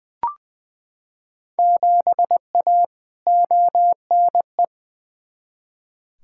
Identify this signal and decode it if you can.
Morse code, heard as a keyed tone, about 20 wpm, 700 Hz. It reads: 7A ONE
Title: Field Day Exchanges Encore - 20wpm